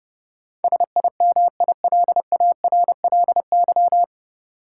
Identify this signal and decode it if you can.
Morse code, heard as a keyed tone, about 30 wpm, 700 Hz. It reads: SIMILARLY
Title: Sentences from Top 1400 Words - New Word Drill - 30wpm